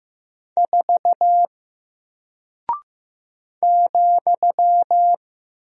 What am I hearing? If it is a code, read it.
Morse code, heard as a keyed tone, about 15 wpm, 700 Hz. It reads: 4 ,
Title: Single Character (Letter:Number:Punctuation:Prosign) Alt 15wpm